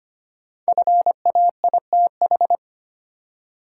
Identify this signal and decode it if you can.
Morse code, heard as a keyed tone, about 25 wpm, 700 Hz. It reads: FAITH
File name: 5 Letter Words - First 3 Letters Given - 25wpm